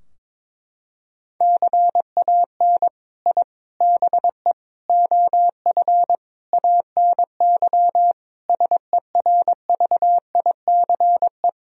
Morse code, heard as a keyed tone, about 21 wpm, 700 Hz. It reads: CAN I BE OF ANY SERVICE
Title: Sentences from Top 1000 Words - Review Part 4 - 22wpm